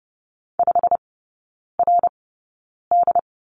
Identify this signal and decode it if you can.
Morse code, heard as a keyed tone, about 30 wpm, spaced as for 9 wpm, 700 Hz. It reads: 5RD
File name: Lesson 18e - Letter m - Three Characters 30wpm